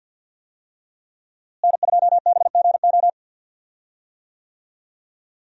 Morse code, heard as a keyed tone, about 46 wpm, 700 Hz. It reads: N2BGO
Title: USA Call Signs - 5 Characters - Repeated 3x - 50wpm